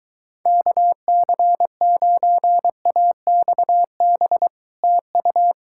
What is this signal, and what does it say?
Morse code, heard as a keyed tone, about 24 wpm, 700 Hz. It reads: KC9AXB TU